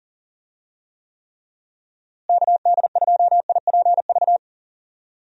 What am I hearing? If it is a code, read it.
Morse code, heard as a keyed tone, about 40 wpm, 700 Hz. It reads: KD2IPV
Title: USA Call Signs - 6 Characters - Repeated 3x - 40wpm